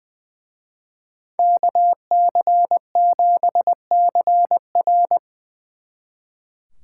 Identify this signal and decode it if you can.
Morse code, heard as a keyed tone, about 24 wpm, 700 Hz. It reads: KC7CR
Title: Speed Racing - USA Call Signs 20wpm